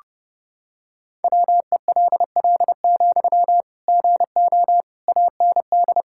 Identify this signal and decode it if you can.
Morse code, heard as a keyed tone, about 29 wpm, 700 Hz. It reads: WELL, GO AND
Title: Sentences from Top 1600 Words 30wpm